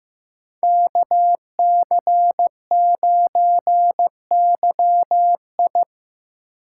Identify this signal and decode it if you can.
Morse code, heard as a keyed tone, about 18 wpm, 700 Hz. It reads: KC9YI
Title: Speed Racing - USA Call Signs 15wpm